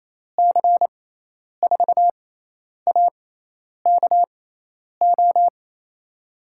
Morse code, heard as a keyed tone, about 29 wpm, spaced as for 10 wpm, 700 Hz. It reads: C4AKO
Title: USA Call Signs - 6x Character Spacing - 28wpm